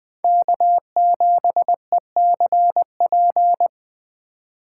Morse code, heard as a keyed tone, about 20 wpm, 700 Hz. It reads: K7ECP
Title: Speed Racing - USA Call Signs 17wpm